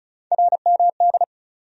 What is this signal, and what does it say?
Morse code, heard as a keyed tone, about 36 wpm, 700 Hz. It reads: RMD
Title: USA Call Sign Postfixes - 3 Letters 35wpm